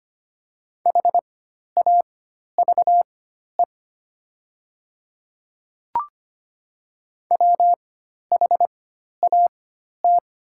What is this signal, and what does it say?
Morse code, heard as a keyed tone, about 25 wpm, spaced as for 12 wpm, 700 Hz. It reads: HAVE WHAT
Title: 4x Character Spacing - Top 100 Words - 25wpm